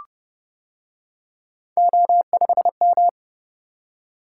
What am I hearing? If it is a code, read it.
Morse code, heard as a keyed tone, about 29 wpm, 700 Hz. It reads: O5M